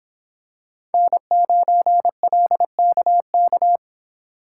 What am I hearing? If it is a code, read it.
Morse code, heard as a keyed tone, about 25 wpm, 700 Hz. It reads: N9LKK